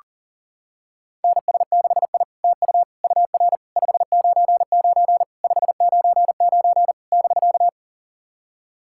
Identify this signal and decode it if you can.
Morse code, heard as a keyed tone, about 40 wpm, 700 Hz. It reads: NS6I TU UR 599 599 <BK>
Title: SOTA QSO Practice 40wpm